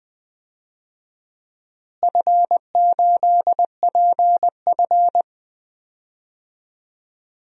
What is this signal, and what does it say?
Morse code, heard as a keyed tone, about 20 wpm, 700 Hz. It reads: F8PF
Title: DXCC-100 Call Signs - 4 Characters - Repeated 2x - 20wpm